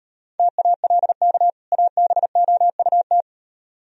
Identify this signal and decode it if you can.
Morse code, heard as a keyed tone, about 38 wpm, 700 Hz. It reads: TALK ABOUT